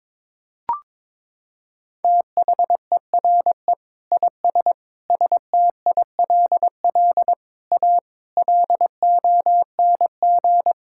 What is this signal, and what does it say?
Morse code, heard as a keyed tone, about 21 wpm, 700 Hz. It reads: THERE IS STILL A LONG
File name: practice / 22wpm / Sentences with 8-Word Phrases - 22wpm